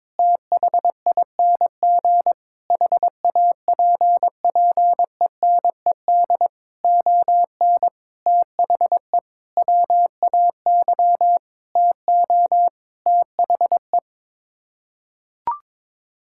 Morse code, heard as a keyed tone, about 22 wpm, 700 Hz. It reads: THING HAPPENED ON THE WAY TO THE
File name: Sets of 9 Words - 22wpm